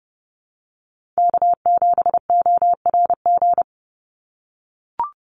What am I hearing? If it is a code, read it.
Morse code, heard as a keyed tone, about 30 wpm, 700 Hz. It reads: K7ORG
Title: NA SOTA Chaser Callsigns 30wpm